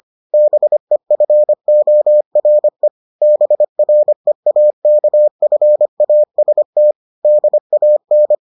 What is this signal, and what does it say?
Morse code, heard as a keyed tone, about 25 wpm, 600 Hz. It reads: BEFORE BREAKFAST DAN